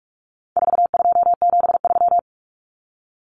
Morse code, heard as a keyed tone, about 44 wpm, 700 Hz. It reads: 4273
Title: Numbers - 4-Digit 45wpm